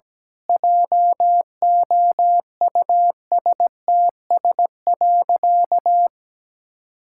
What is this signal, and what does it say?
Morse code, heard as a keyed tone, about 17 wpm, 700 Hz. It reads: JOUSTS.
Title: Pangram Challenge v2 - 17wpm